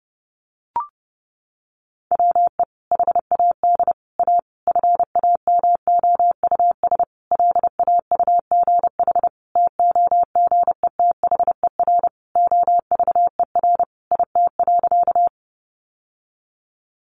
Morse code, heard as a keyed tone, about 30 wpm, 700 Hz. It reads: WE HAD A FAMOUS LAUGH TOGETHER OVER IT.
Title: Sentences from Top 1500 Words 30wpm